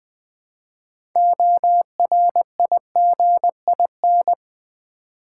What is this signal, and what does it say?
Morse code, heard as a keyed tone, about 20 wpm, 700 Hz. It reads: ORIGIN
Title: 6 Letter Words - First 3 Letters Given - 20wpm